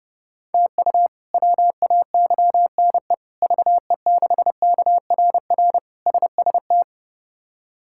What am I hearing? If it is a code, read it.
Morse code, heard as a keyed tone, about 30 wpm, 700 Hz. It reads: TU WAYNE VE6KRR SST
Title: K1USN SST QSO Practice - 30wpm